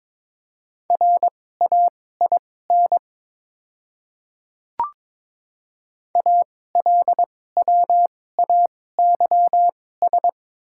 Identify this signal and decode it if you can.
Morse code, heard as a keyed tone, about 22 wpm, spaced as for 16 wpm, 700 Hz. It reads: RAIN ALWAYS
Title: 2x Character Spacing - Top 300 Words - 22wpm